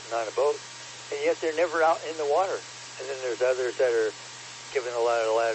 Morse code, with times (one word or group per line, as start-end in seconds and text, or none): none